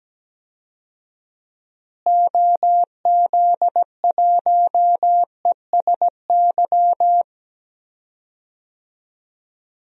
Morse code, heard as a keyed tone, about 17 wpm, 700 Hz. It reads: OZ1ESY
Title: DXCC-100 Call Signs - 6 Characters - Repeated 2x - 17wpm